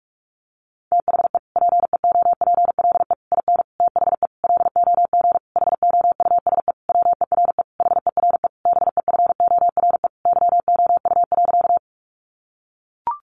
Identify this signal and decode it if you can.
Morse code, heard as a keyed tone, about 44 wpm, 700 Hz. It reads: THE PEOPLE IN THE LOG HOUSE WERE HERE BEFORE YOU.
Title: Sentences from Top 1400 Words 45wpm